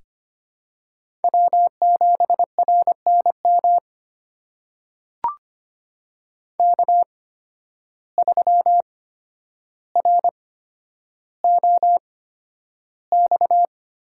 Morse code, heard as a keyed tone, about 25 wpm, spaced as for 11 wpm, 700 Hz. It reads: W7RNM K3ROX
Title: USA Call Signs - 5 Characters - 8x Character Spacing - 25wpm